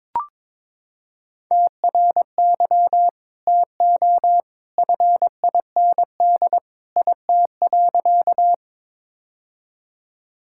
Morse code, heard as a keyed tone, about 22 wpm, 700 Hz. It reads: TRY TO FIND IT.
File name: practice / 22wpm / Sentences with 2-Word Phrases - 22wpm